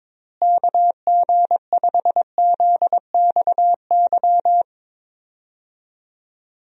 Morse code, heard as a keyed tone, about 22 wpm, 700 Hz. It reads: KG5ZXY
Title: USA Call Signs - 6 Characters - Repeated 2x - 22wpm